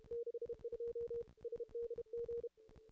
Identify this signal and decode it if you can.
Morse code, heard as a keyed tone, about 30 wpm, 440 Hz. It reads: 62 SDG
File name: contests / ss-cw-k5tr-op-n5rz -2300